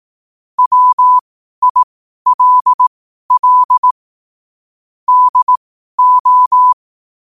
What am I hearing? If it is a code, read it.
Morse code, heard as a keyed tone, about 18 wpm, spaced as for 12 wpm, 1000 Hz. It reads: WILL DO
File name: qotdfiles / 12wpm